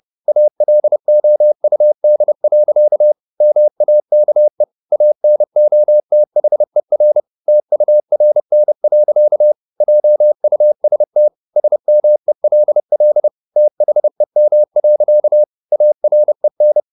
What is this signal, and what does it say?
Morse code, heard as a keyed tone, about 30 wpm, 600 Hz. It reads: ALOUD. MAKE ANOTHER TURN. JUST SMELL THEM. AREN
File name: the old man and the sea / the old man and the sea 1 30wpm